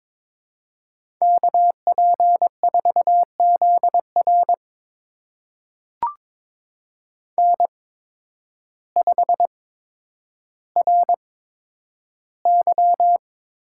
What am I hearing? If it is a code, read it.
Morse code, heard as a keyed tone, about 22 wpm, spaced as for 10 wpm, 700 Hz. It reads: KP4ZR N5RY
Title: USA Call Signs - 8x Character Spacing - 22wpm